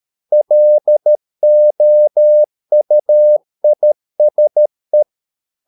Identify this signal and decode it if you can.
Morse code, heard as a keyed tone, about 13 wpm, 600 Hz. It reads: LOUISE
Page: Ferie 13 wpm